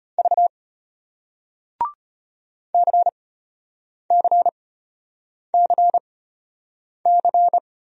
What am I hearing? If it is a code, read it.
Morse code, heard as a keyed tone, about 34 wpm, spaced as for 8 wpm, 700 Hz. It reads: V CCCC